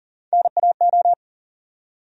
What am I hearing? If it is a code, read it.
Morse code, heard as a keyed tone, about 40 wpm, 700 Hz. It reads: NAO
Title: USA Call Sign Postfixes - 3 Letters 40wpm